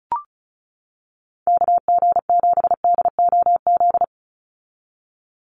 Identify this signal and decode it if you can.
Morse code, heard as a keyed tone, about 34 wpm, 700 Hz. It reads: KG7DOZ
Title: USA Call Signs 35wpm